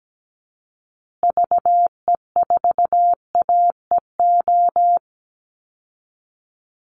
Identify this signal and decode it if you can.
Morse code, heard as a keyed tone, about 17 wpm, 700 Hz. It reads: VE4AEO
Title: Call Signs Repeated 2x - 17wpm